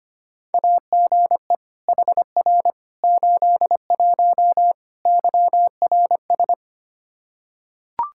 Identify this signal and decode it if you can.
Morse code, heard as a keyed tone, about 25 wpm, 700 Hz. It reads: AGE HR 81 YRS
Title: Practice QSO - Exchange 3 - More Info - 25wpm